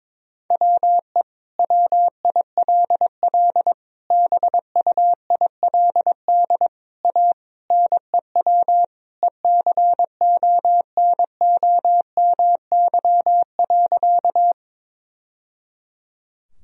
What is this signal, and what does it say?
Morse code, heard as a keyed tone, about 22 wpm, 700 Hz. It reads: WE WILL BUILD A NEW ECONOMY.
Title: Sentences from Top 1000 Words - Review Part 3 - 22wpm